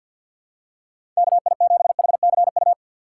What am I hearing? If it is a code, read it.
Morse code, heard as a keyed tone, about 47 wpm, 700 Hz. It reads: KI7HCU
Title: USA Call Signs - 6 Characters - 4x Character Spacing - 50wpm